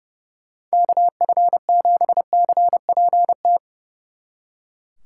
Morse code, heard as a keyed tone, about 37 wpm, 700 Hz. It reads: KF7CPT
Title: Speed Racing - USA Call Signs 30wpm